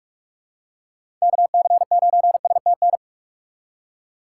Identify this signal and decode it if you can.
Morse code, heard as a keyed tone, about 44 wpm, 700 Hz. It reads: KC9STN
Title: USA Call Signs - 3x Character Spacing - 45wpm